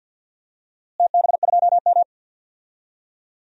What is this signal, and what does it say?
Morse code, heard as a keyed tone, about 50 wpm, 700 Hz. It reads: TB2M
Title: DXCC-100 Call Signs - 4 Characters - Repeated 2x - 50wpm